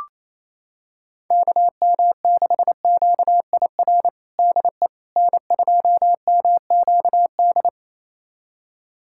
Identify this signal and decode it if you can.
Morse code, heard as a keyed tone, about 28 wpm, 700 Hz. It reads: KM6QIR DE N2MQD
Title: USA Callsign DE Callsign 28wpm